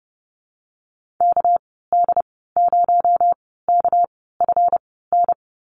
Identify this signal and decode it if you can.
Morse code, heard as a keyed tone, about 30 wpm, spaced as for 17 wpm, 700 Hz. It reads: KD0KFN
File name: Lesson 37g - Letter x - Call Signs 30wpm